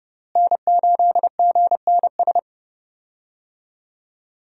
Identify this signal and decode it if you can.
Morse code, heard as a keyed tone, about 29 wpm, 700 Hz. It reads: N8GNS
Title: USA Call Signs - 5 Characters - Repeated 2x - 30wpm